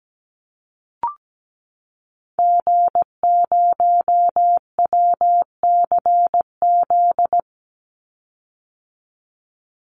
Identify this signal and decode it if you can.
Morse code, heard as a keyed tone, about 17 wpm, 700 Hz. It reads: G0WCZ
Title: Call Signs - 17wpm